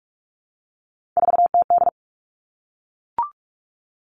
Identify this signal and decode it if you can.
Morse code, heard as a keyed tone, about 44 wpm, 700 Hz. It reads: 4TD
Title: All Cut Numbers - 3-Digit 45wpm